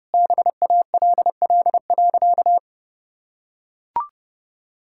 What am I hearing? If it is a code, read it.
Morse code, heard as a keyed tone, about 30 wpm, 700 Hz. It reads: BALL.
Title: Sentences from Top 1000 Words - Review Part 6 - 30wpm